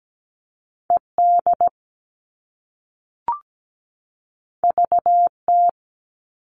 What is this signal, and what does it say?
Morse code, heard as a keyed tone, about 17 wpm, 700 Hz. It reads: ED VT
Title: All Cut Numbers - 2-Digit 17wpm